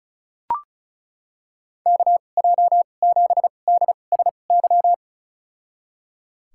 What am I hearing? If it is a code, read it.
Morse code, heard as a keyed tone, about 34 wpm, spaced as for 25 wpm, 700 Hz. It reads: KJ7DSY
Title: USA Call Signs - 6 Characters - 2x Character Spacing - 35wpm